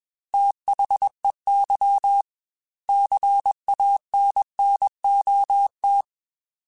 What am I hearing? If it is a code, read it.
Morse code, heard as a keyed tone, about 21 wpm, 800 Hz. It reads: THEY CANNOT
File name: Quotes / Quotes-mp3s / Advice @20